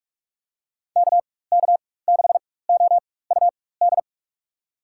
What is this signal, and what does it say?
Morse code, heard as a keyed tone, about 44 wpm, spaced as for 21 wpm, 700 Hz. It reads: KK6OUD
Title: USA Call Signs - 6 Characters - 4x Character Spacing - 45wpm